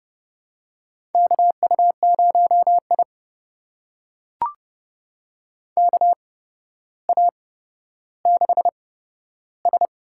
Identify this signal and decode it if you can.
Morse code, heard as a keyed tone, about 31 wpm, spaced as for 13 wpm, 700 Hz. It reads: KU0I KA6S
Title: USA Call Signs - 4 Characters - 8x Character Spacing - 30wpm